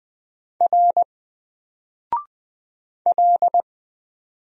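Morse code, heard as a keyed tone, about 20 wpm, 700 Hz. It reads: R L
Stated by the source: Single letter and number Morse drill at 20 wpm, from N4LCD.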